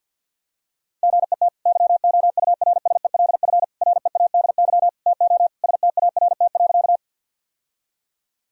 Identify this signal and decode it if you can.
Morse code, heard as a keyed tone, about 51 wpm, 700 Hz. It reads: GET YOURSELF READY TO START.
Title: Sentences from Top 1000 Words - Review Part 1 - 50wpm